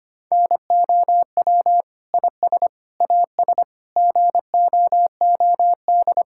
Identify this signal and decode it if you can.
Morse code, heard as a keyed tone, about 25 wpm, 700 Hz. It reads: NOW IS AS GOOD
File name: Sets of 9 Words - 25wpm